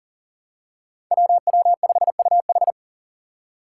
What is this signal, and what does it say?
Morse code, heard as a keyed tone, about 40 wpm, 700 Hz. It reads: WW5UH
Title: Five Challenging Letter-Number Combinations 40wpm